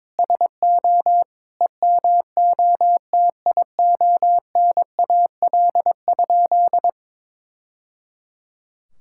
Morse code, heard as a keyed tone, about 22 wpm, 700 Hz. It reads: SO EMOTIONAL?
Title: Sentences from Top 1000 Words - Review Part 5 - 22wpm